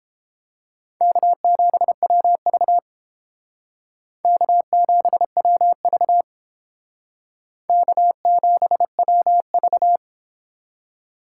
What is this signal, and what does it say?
Morse code, heard as a keyed tone, about 29 wpm, 700 Hz. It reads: K7WV K7WV K7WV